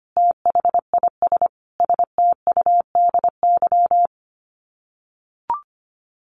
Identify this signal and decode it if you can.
Morse code, heard as a keyed tone, about 25 wpm, 700 Hz. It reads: THIS STUDY
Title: Sets of 5 Words Encore2 25wpm